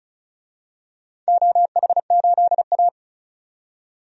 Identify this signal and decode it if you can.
Morse code, heard as a keyed tone, about 36 wpm, 700 Hz. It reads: OH8A
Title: DXCC-100 Call Signs - 4 Characters - 35wpm